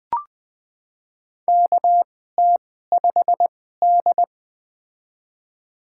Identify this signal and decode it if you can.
Morse code, heard as a keyed tone, about 20 wpm, spaced as for 15 wpm, 700 Hz. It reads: KT5D
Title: USA Call Signs - 4 Characters - 2x Character Spacing - 20wpm